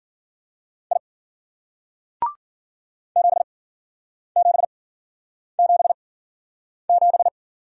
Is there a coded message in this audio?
Yes: Morse code from 0.9 to 7.3 s, heard as a keyed tone, about 53 wpm, spaced as for 9 wpm, 700 Hz. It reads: I 7777